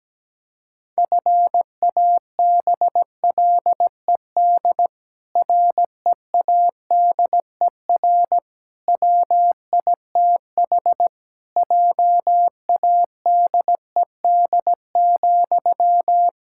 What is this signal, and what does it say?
Morse code, heard as a keyed tone, about 17 wpm, 700 Hz. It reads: FABLED READER WITH JADED,
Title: Pangram Challenge v2 - 17wpm